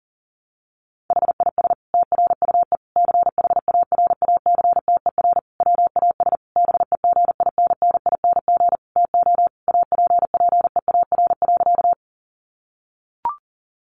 Morse code, heard as a keyed tone, about 40 wpm, 700 Hz. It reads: HIS TRUE CHARACTER WAS BEGINNING TO APPEAR.
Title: Sentences from Top 1200 Words 40wpm